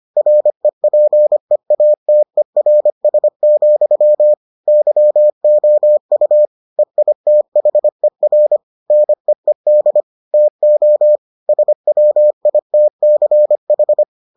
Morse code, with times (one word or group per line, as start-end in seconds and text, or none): 0.2-4.3 REPEATERS,
4.7-6.5 YOU
6.8-8.6 EITHER
8.9-10.0 NEED
10.3-11.2 TO
11.5-14.0 SWITCH